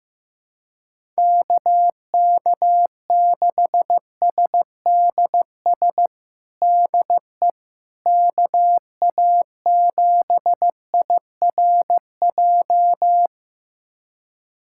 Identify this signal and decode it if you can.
Morse code, heard as a keyed tone, about 15 wpm, 700 Hz. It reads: KK6SDS DE KA7IRJ
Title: USA Callsign DE Callsign - 6 Characters - 15wpm